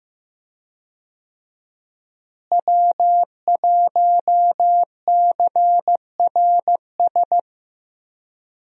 Morse code, heard as a keyed tone, about 15 wpm, 700 Hz. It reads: W1CRS
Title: USA Call Signs - 5 Characters - Repeated 3x - 15wpm